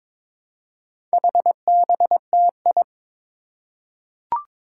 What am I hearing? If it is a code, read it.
Morse code, heard as a keyed tone, about 22 wpm, 700 Hz. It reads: HBTI
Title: Four Challenging Letter-Number Combinations 22wpm